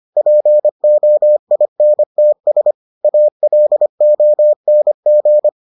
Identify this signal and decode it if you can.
Morse code, heard as a keyed tone, about 25 wpm, 600 Hz. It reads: POINTS ALONG